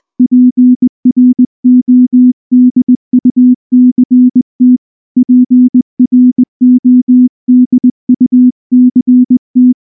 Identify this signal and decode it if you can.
Morse code, heard as a keyed tone, about 19 wpm, 250 Hz. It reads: PRODUCT PRODUCT
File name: morse / morse-training-20WPM-10